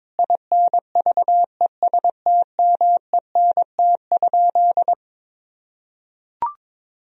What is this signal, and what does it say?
Morse code, heard as a keyed tone, about 22 wpm, 700 Hz. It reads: INVESTMENT?
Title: Sentences from Top 1000 Words - Review Part 4 - 22wpm